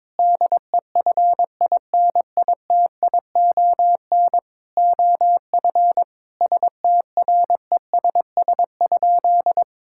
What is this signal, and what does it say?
Morse code, heard as a keyed tone, about 22 wpm, 700 Hz. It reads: DEFINITION OF STRESS?
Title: Sentences from Top 1000 Words - Review Part 6 - 22wpm